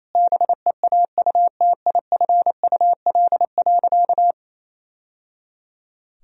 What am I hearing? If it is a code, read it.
Morse code, heard as a keyed tone, about 28 wpm, 700 Hz. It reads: BEAUTIFUL.
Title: Sentences from Top 1000 Words - Review Part 1 - 28wpm